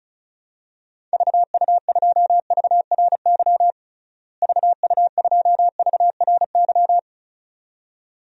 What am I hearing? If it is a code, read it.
Morse code, heard as a keyed tone, about 34 wpm, 700 Hz. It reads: VU2VRY VU2VRY
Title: DXCC-100 Call Signs - 6 Characters - Repeated 2x - 35wpm